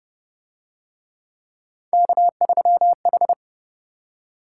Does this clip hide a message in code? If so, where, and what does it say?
Yes: Morse code from 1.9 to 3.3 s, heard as a keyed tone, about 29 wpm, 700 Hz. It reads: K3H